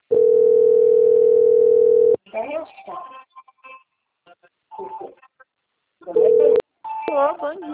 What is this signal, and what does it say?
Ringback tone (North America), 440 + 480 Hz, with speech.
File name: out- - - - - .20227